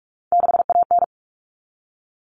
Morse code, heard as a keyed tone, about 44 wpm, 700 Hz. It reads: 6AN
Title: All Cut Numbers - 3-Digit 45wpm